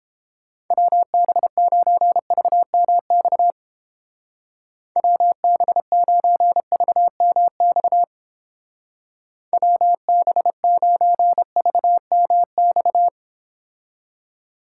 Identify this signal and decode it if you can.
Morse code, heard as a keyed tone, about 31 wpm, 700 Hz. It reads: WB9VMX WB9VMX WB9VMX